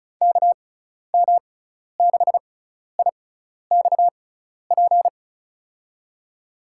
Morse code, heard as a keyed tone, about 34 wpm, spaced as for 12 wpm, 700 Hz. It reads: KM6IXP
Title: USA Call Signs - 6 Characters - 6x Character Spacing - 35wpm